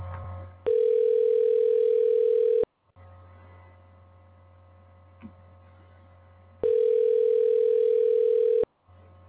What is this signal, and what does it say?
Ringback tone (North America), 440 + 480 Hz.